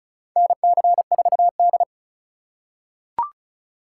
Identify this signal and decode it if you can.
Morse code, heard as a keyed tone, about 36 wpm, 700 Hz. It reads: NC4D
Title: USA Call Signs - 4 Characters - 4x Character Spacing - 35wpm